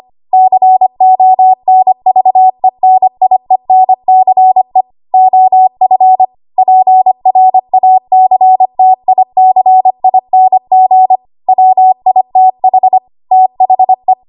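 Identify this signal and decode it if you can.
Morse code, heard as a keyed tone, about 25 wpm, 750 Hz. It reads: CONVENIENCE OF PRACTICING WITH THE
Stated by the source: ARRL 25 WPM code practice archive.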